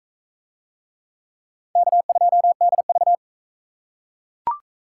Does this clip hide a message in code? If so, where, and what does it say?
Yes: Morse code from 1.8 to 3.1 s, heard as a keyed tone, about 36 wpm, 700 Hz. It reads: K2DV